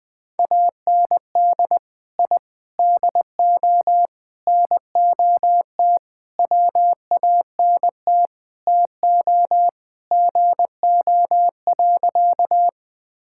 Morse code, heard as a keyed tone, about 19 wpm, 700 Hz. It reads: AND I DO NOT WANT TO GO.
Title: Sentences with 5-Word Phrases - 20wpm